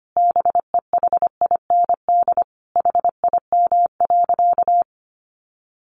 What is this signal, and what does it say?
Morse code, heard as a keyed tone, about 25 wpm, 700 Hz. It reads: BEHIND HIM.
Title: Sentences from Top 700 Words 25wpm